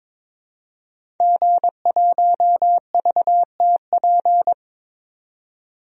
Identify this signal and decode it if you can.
Morse code, heard as a keyed tone, about 22 wpm, 700 Hz. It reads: G1VTP
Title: DXCC-100 Call Signs - 5 Characters - 22wpm